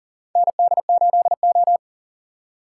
Morse code, heard as a keyed tone, about 40 wpm, 700 Hz. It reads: ND8O
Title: USA Call Signs - 4 Characters - 2x Character Spacing - 40wpm